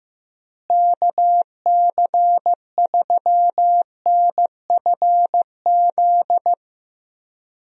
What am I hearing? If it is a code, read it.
Morse code, heard as a keyed tone, about 15 wpm, 700 Hz. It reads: KC3NFZ
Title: USA Call Signs - 6 Characters - Repeated 3x - 15wpm